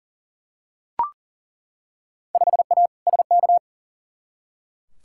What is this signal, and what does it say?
Morse code, heard as a keyed tone, about 40 wpm, 700 Hz. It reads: 5A SK
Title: Field Day Exchanges Encore - 40wpm